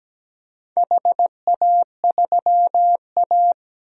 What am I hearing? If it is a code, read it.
Morse code, heard as a keyed tone, about 17 wpm, 700 Hz. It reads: HA3A
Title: DXCC-100 Call Signs - 4 Characters - 17wpm